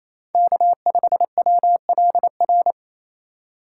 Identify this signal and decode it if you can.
Morse code, heard as a keyed tone, about 28 wpm, 700 Hz. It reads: K5WLR
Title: USA Call Signs - 5 Characters - 4x Character Spacing - 28wpm